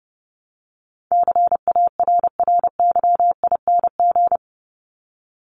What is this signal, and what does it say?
Morse code, heard as a keyed tone, about 30 wpm, 700 Hz. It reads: CARRYING
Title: ING Suffix - Top 100 Words with Base Words 30wpm